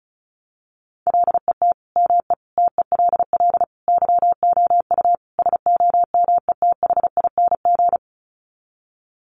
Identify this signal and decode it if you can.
Morse code, heard as a keyed tone, about 34 wpm, 700 Hz. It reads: LET ME TELL YOU SOMETHING
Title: Sets of 5 Words 35wpm